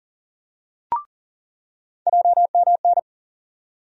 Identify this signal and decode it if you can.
Morse code, heard as a keyed tone, about 40 wpm, 700 Hz. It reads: JMN
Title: USA Call Sign Postfixes - 3 Letters 40wpm